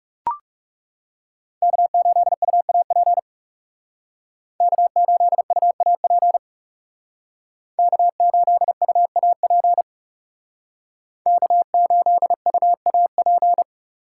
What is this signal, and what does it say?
Morse code, heard as a keyed tone, about 36 wpm, 700 Hz. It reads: K8UAP K8UAP K8UAP K8UAP
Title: Speed Racing - USA 1x3 Call Signs 30wpm